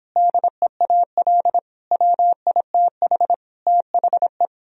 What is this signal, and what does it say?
Morse code, heard as a keyed tone, about 24 wpm, 700 Hz. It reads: DEAL WITH THE